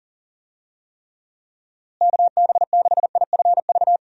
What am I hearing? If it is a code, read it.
Morse code, heard as a keyed tone, about 40 wpm, 700 Hz. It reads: KB6IFV
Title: USA Call Signs - 6 Characters - Repeated 3x - 40wpm